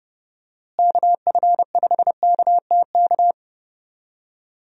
Morse code, heard as a keyed tone, about 31 wpm, 700 Hz. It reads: KF5KTK
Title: USA Call Signs - 6 Characters - Repeated 3x - 30wpm